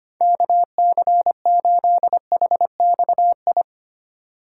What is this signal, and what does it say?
Morse code, heard as a keyed tone, about 25 wpm, 700 Hz. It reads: KC8HXI
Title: USA Call Signs - 6 Characters - 3x Character Spacing - 25wpm